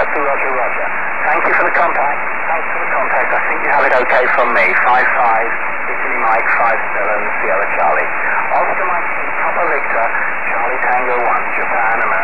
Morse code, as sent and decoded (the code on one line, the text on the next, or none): none